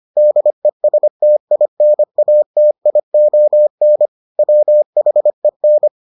Morse code, heard as a keyed tone, about 25 wpm, 600 Hz. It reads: DESTINATION WHEN